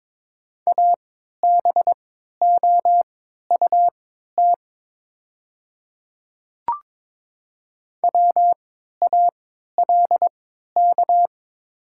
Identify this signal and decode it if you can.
Morse code, heard as a keyed tone, about 22 wpm, spaced as for 13 wpm, 690 Hz. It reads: ABOUT WALK
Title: 3x Character Spacing - Top 300 Words - 22wpm